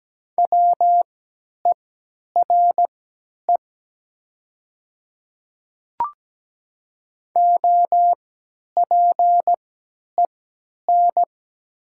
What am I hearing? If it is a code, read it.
Morse code, heard as a keyed tone, about 17 wpm, spaced as for 10 wpm, 690 Hz. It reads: WERE OPEN
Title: 3x Character Spacing - Top 500 Words - 17wpm